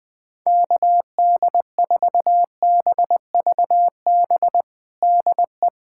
Morse code, heard as a keyed tone, about 20 wpm, 700 Hz. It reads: KD4BVB DE
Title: USA Callsign DE Callsign - 6 Characters - 20wpm